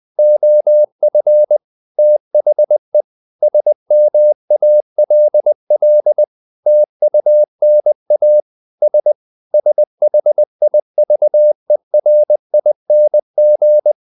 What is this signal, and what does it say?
Morse code, heard as a keyed tone, about 20 wpm, 600 Hz. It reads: OF THE SMALL TUNA S SHIVERING
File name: the old man and the sea / the old man and the sea part 1 20wpm